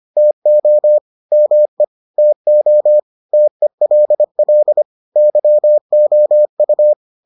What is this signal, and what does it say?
Morse code, heard as a keyed tone, about 25 wpm, 600 Hz. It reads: TO ME TO TELL YOU